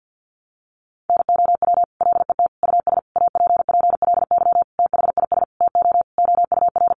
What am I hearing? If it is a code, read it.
Morse code, heard as a keyed tone, about 50 wpm, 700 Hz. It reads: NOW LET US APPLY THIS TO OUR
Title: Sentences from Top 1200 Words 50wpm